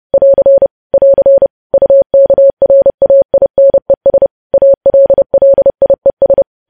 Morse code, heard as a keyed tone, about 30 wpm, 550 Hz. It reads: <AR> <AR> UKRAINES ALLIES
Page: news headlines in morse code at 30 wpm